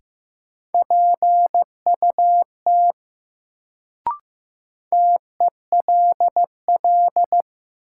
Morse code, heard as a keyed tone, about 15 wpm, 700 Hz. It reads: PUT TELL
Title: Sentences from Top 200 Words - New Word Drill - 15wpm